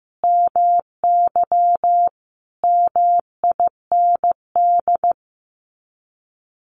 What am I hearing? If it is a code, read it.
Morse code, heard as a keyed tone, about 15 wpm, 700 Hz. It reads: MY MIND
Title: Sets of 5 Words Encore2 15wpm